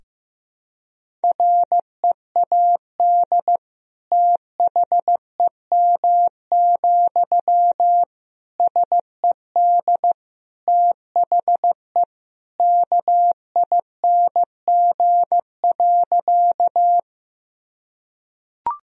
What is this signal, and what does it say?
Morse code, heard as a keyed tone, about 15 wpm, 700 Hz. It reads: READ THEM, SED THE KING.
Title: Sentences from Top 400 Words and Abbreviations 15wpm